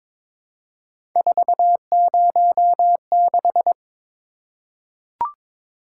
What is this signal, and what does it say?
Morse code, heard as a keyed tone, about 22 wpm, 700 Hz. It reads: 406
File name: practice / 22wpm / Three Letter-Number Combinations - 22wpm